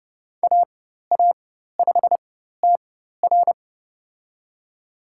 Morse code, heard as a keyed tone, about 30 wpm, spaced as for 14 wpm, 700 Hz. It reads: AA5TR
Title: USA Call Signs - 5 Characters - 4x Character Spacing - 30wpm